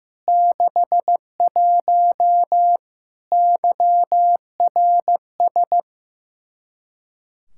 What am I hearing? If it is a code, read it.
Morse code, heard as a keyed tone, about 15 wpm, 700 Hz. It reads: 61 YRS
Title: Practice QSO - Exchange 3 - More Info - 15wpm